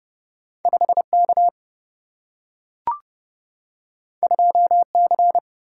Morse code, heard as a keyed tone, about 30 wpm, 700 Hz. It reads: 5 K 2 C
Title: Single-Letter Single-Number 30wpm